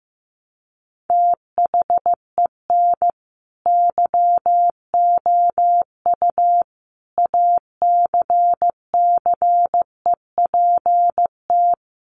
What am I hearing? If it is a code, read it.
Morse code, heard as a keyed tone, about 15 wpm, 700 Hz. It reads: THEN YOU ACCEPT
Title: Sentences from Top 1300 Words 15wpm